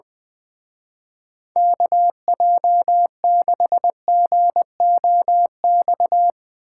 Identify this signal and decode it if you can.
Morse code, heard as a keyed tone, about 22 wpm, 700 Hz. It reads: KJ6GOX
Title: Speed Racing - USA 2x3 Call Signs 20wpm